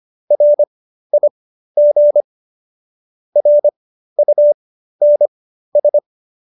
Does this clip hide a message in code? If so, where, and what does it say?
Yes: Morse code from 0.3 to 6.0 s, heard as a keyed tone, about 25 wpm, spaced as for 13 wpm, 590 Hz. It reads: RIG RUNS